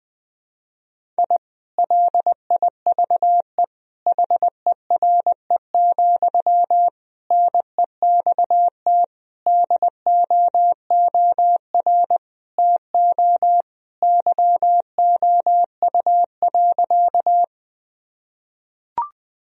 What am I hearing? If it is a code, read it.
Morse code, heard as a keyed tone, about 20 wpm, 700 Hz. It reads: I LIVE HERE, NEXT DOOR TO YOU.
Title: Sentences from Top 1000 Words - Review Part 1 - 20wpm